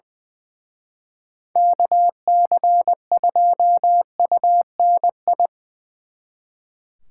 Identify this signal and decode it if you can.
Morse code, heard as a keyed tone, about 23 wpm, 700 Hz. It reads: KC2UNI
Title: Speed Racing - USA 2x3 Call Signs 20wpm